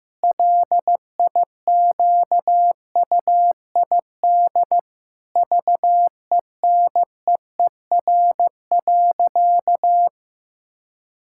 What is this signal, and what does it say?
Morse code, heard as a keyed tone, about 15 wpm, 700 Hz. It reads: LIQUID VENEER.
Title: Pangram Challenge v2 - 15wpm